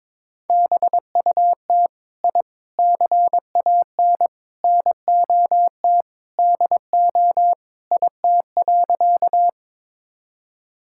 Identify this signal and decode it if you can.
Morse code, heard as a keyed tone, about 21 wpm, 700 Hz. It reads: BUT I CAN NOT DO IT.
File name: practice / 22wpm / Sentences with 4-Word Phrases - 22wpm